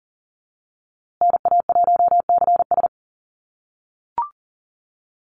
Morse code, heard as a keyed tone, about 40 wpm, 700 Hz. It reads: NA1CS
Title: USA Call Signs Repeated 2x 40wpm